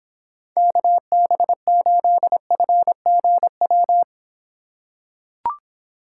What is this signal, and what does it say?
Morse code, heard as a keyed tone, about 25 wpm, 700 Hz. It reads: KB8FGW